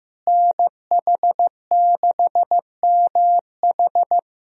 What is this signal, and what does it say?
Morse code, heard as a keyed tone, about 15 wpm, 700 Hz. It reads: NH6MH
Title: USA Call Signs - 5 Characters - Repeated 2x - 15wpm